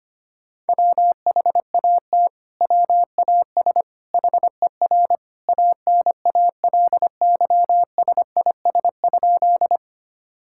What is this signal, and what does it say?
Morse code, heard as a keyed tone, about 25 wpm, 700 Hz. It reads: WHAT WAS HER ANALYSIS?
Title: Sentences from Top 1000 Words - Review Part 5 - 25wpm